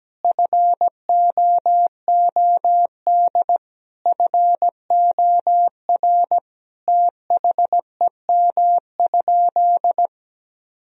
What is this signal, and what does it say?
Morse code, heard as a keyed tone, about 17 wpm, 700 Hz. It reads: FOOD FOR THEM?
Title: Sentences from Top 1000 Words - Review Part 1 - 17wpm